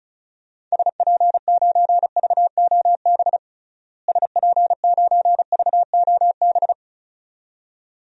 Morse code, heard as a keyed tone, about 36 wpm, 700 Hz. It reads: SP9VOB SP9VOB
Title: DXCC-100 Call Signs - 6 Characters - Repeated 2x - 35wpm